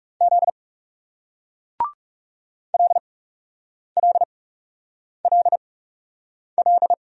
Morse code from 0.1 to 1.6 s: Z

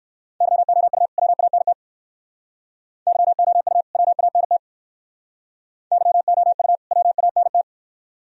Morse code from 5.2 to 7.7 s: YOU WANT

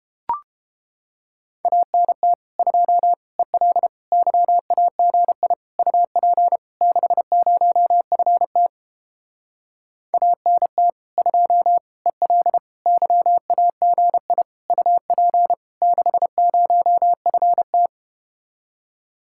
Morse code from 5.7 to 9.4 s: UP 60FT

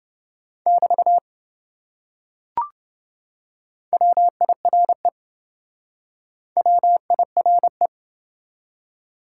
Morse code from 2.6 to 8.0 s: WIRE WIRE

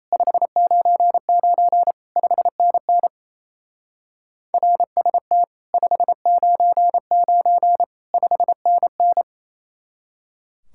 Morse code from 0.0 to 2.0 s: 599

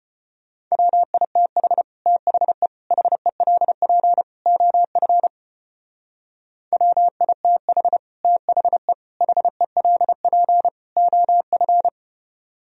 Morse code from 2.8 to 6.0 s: HELP OF

